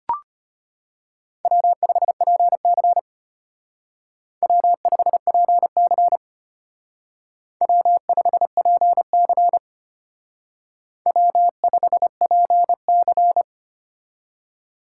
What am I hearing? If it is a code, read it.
Morse code, heard as a keyed tone, about 30 wpm, 700 Hz. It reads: W5PC W5PC W5PC W5PC
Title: Speed Racing - USA 1x2 Call Signs 25wpm